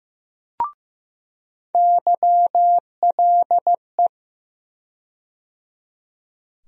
Morse code, heard as a keyed tone, about 15 wpm, 700 Hz. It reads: YLE
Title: USA Call Sign Postfixes - 3 Letters 15wpm